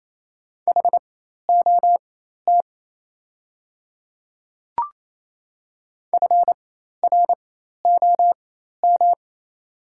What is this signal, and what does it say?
Morse code, heard as a keyed tone, about 28 wpm, spaced as for 13 wpm, 700 Hz. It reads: HOT FROM
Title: 4x Character Spacing - Top 100 Words - 28wpm